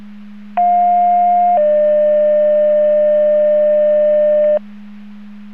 Two-tone page, heard as a one-second tone, then a three-second tone, about 710 then 600 Hz. It is heard from 0.6 to 4.6 s.